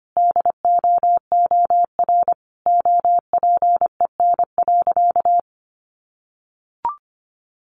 Morse code, heard as a keyed tone, about 25 wpm, 700 Hz. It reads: DOOR OPEN.
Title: Sentences from Top 1000 Words 25wpm